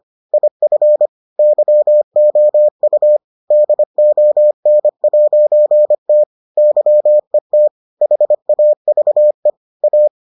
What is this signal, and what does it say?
Morse code, heard as a keyed tone, about 25 wpm, 600 Hz. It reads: IF YOU DON'T YET HAVE A